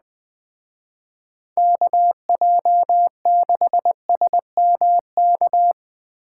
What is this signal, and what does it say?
Morse code, heard as a keyed tone, about 20 wpm, 700 Hz. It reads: KJ6SMK